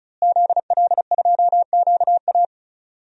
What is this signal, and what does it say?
Morse code, heard as a keyed tone, about 34 wpm, 700 Hz. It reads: ZL2QA
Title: DXCC-100 Call Signs - 5 Characters - Repeated 2x - 35wpm